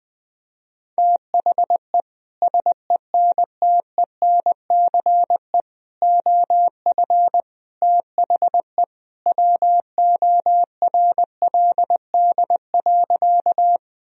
Morse code, heard as a keyed tone, about 20 wpm, 700 Hz. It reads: THE SENTENCE OF THE WORLD.
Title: Sentences from Top 1000 Words - Review Part 1 - 20wpm